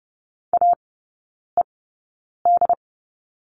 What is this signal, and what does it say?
Morse code, heard as a keyed tone, about 30 wpm, spaced as for 9 wpm, 700 Hz. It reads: AED
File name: Lesson 17e - Letter u - Three Characters 30wpm